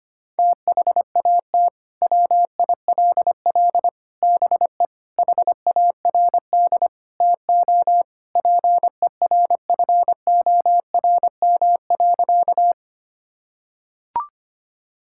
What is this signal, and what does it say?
Morse code, heard as a keyed tone, about 25 wpm, 700 Hz. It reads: THAT WILL BE HARD TO PERFORM.
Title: Sentences from Top 1000 Words - Review Part 5 - 25wpm